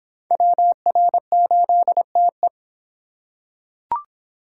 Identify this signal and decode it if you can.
Morse code, heard as a keyed tone, about 24 wpm, 700 Hz. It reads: WR8TE